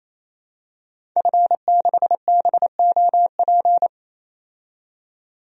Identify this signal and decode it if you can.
Morse code, heard as a keyed tone, about 28 wpm, 700 Hz. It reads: F6BOP
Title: DXCC-100 Call Signs - 5 Characters - 28wpm